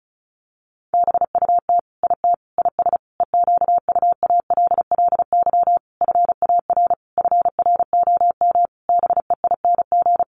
Morse code, heard as a keyed tone, about 34 wpm, 700 Hz. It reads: BUT IT IS EQUALLY FAR FROM BEING
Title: Sentences from Top 1400 Words 35wpm